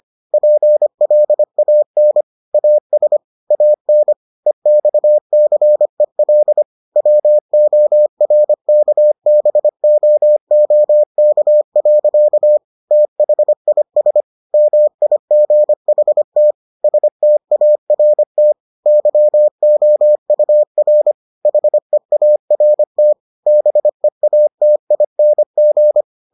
Morse code, heard as a keyed tone, about 25 wpm, 600 Hz. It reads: PLAN AS AN EXCEL WORKBOOK. THIS MIGHT START YOUR HEART BEATING